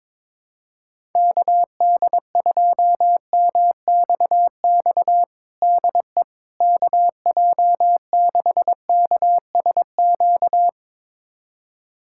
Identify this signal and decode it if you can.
Morse code, heard as a keyed tone, about 22 wpm, 700 Hz. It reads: KD2MXX DE KJ6KSQ
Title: USA Callsign DE Callsign 22wpm